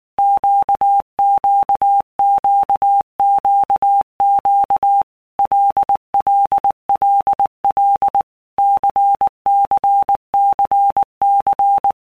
Morse code, heard as a keyed tone, about 19 wpm, 800 Hz. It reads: QQQQQ LLLL CCCC
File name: Basic Letter Groups By Blocks / Groups@18 / LFCYQ@18